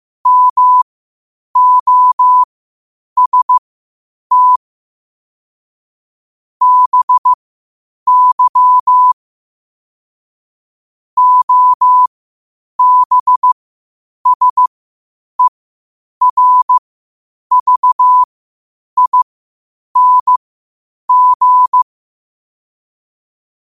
Morse code, heard as a keyed tone, about 15 wpm, spaced as for 8 wpm, 1000 Hz. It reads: MOST BY OBSERVING
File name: qotdfiles / 8wpm